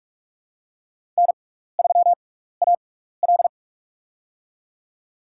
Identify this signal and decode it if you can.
Morse code, heard as a keyed tone, about 44 wpm, spaced as for 15 wpm, 700 Hz. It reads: N3AL
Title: USA Call Signs - 6x Character Spacing - 45wpm